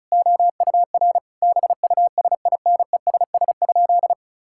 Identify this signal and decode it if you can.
Morse code, heard as a keyed tone, about 36 wpm, 700 Hz. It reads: OUR BUSINESS?
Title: Sentences from Top 1000 Words - Review Part 6 - 35wpm